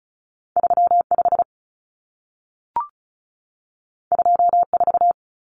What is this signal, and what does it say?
Morse code, heard as a keyed tone, about 36 wpm, 700 Hz. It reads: 35 24
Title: Common Cut Numbers - 2-Digit 35wpm